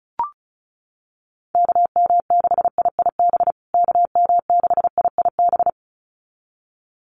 Morse code, heard as a keyed tone, about 34 wpm, 700 Hz. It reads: KM6IIB KM6IIB
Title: USA Call Signs Repeated 2x 35wpm